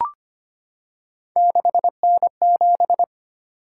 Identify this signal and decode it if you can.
Morse code, heard as a keyed tone, about 25 wpm, 700 Hz. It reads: 6N7
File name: Three Letter-Number Combinations - 25wpm